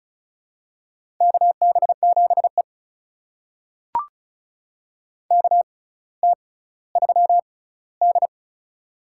Morse code, heard as a keyed tone, about 36 wpm, spaced as for 18 wpm, 700 Hz. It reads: KB7E KT3D
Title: USA Call Signs - 6x Character Spacing - 35wpm